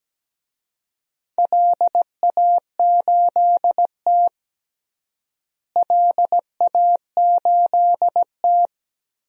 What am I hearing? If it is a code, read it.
Morse code, heard as a keyed tone, about 17 wpm, 700 Hz. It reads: LA8T LA8T
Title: DXCC-100 Call Signs - 4 Characters - Repeated 3x - 17wpm